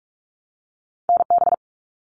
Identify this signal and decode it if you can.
Morse code, heard as a keyed tone, about 46 wpm, 700 Hz. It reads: NB
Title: Canadian Provinces and Territories 45wpm